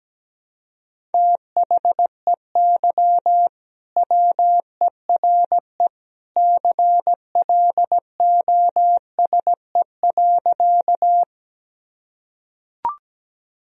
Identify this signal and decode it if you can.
Morse code, heard as a keyed tone, about 17 wpm, 700 Hz. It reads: THEY WERE CLOSE.
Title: Sentences from Top 1000 Words - Review Part 1 - 17wpm